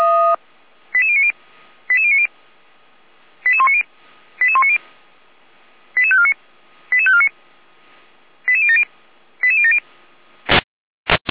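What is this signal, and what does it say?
Five-tone selective call (ZVEI), heard as quick runs of five tones, tones of about 70 ms. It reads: 88090 88090 88190 88190 88549 88549 88070 88070
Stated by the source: ZVEI five-tone selective call received on an SDR.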